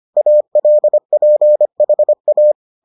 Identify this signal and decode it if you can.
Morse code, heard as a keyed tone, about 25 wpm, 600 Hz. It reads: ALPHA